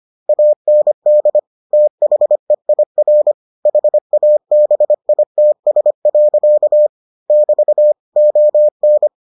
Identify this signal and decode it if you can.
Morse code, heard as a keyed tone, about 25 wpm, 600 Hz. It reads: AND THEIR HABITS. <BT> ON